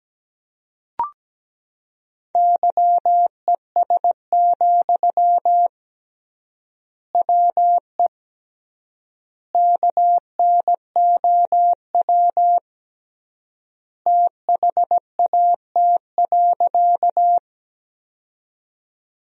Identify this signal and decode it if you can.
Morse code, heard as a keyed tone, about 17 wpm, 700 Hz. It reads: YES, WE KNOW THAT.
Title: 3x Word-Spacing - Sentences from Top 500 Words - 17wpm